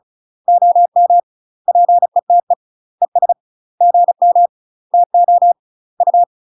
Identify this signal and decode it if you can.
Morse code, heard as a keyed tone, about 34 wpm, 700 Hz. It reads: OM PETE ES GM TO U
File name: UK MASTER QSOs TRAINING / 35WPM / UK MASTER QSOs 0034